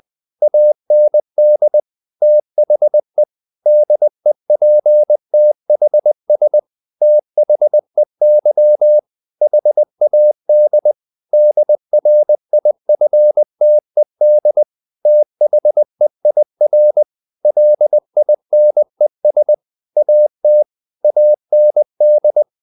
Morse code, heard as a keyed tone, about 20 wpm, 600 Hz. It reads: AND THE DEPTHS THEY HAD DRIFTED THEIR LINES AT AND
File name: the old man and the sea / the old man and the sea part 1 20wpm